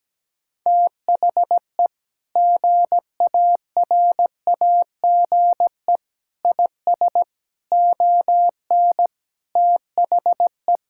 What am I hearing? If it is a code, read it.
Morse code, heard as a keyed tone, about 17 wpm, 700 Hz. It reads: THE GARAGE IS ON THE
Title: Sentences from Top 1600 Words 17wpm